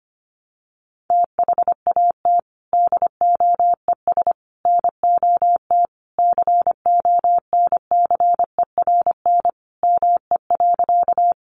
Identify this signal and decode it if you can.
Morse code, heard as a keyed tone, about 25 wpm, 700 Hz. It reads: THAT DOES NOT CONCERN ME.
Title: Sentences from Top 1000 Words 25wpm